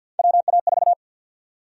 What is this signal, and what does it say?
Morse code, heard as a keyed tone, about 52 wpm, 700 Hz. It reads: WA4
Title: USA Call Sign Prefixes - 50wpm